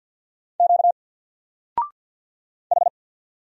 Morse code, heard as a keyed tone, about 50 wpm, 700 Hz. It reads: Q H
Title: Single Letter-Number 50wpm